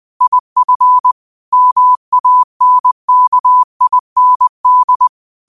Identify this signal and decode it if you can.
Morse code, heard as a keyed tone, about 20 wpm, 1000 Hz. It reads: IF MANKIND
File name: qotdfiles / 20wpm